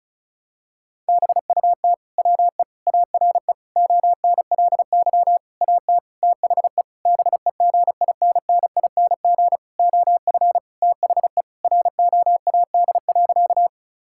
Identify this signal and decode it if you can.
Morse code, heard as a keyed tone, about 36 wpm, 700 Hz. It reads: BUT WE ARE ONLY AT THE BEGINNING OF THE ROAD.
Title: Sentences from Top 1000 Words - Review Part 5 - 35wpm